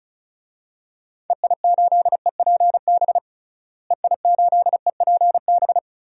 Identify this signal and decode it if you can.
Morse code, heard as a keyed tone, about 36 wpm, 700 Hz. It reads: EI8EPB EI8EPB
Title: DXCC-100 Call Signs - 6 Characters - Repeated 2x - 35wpm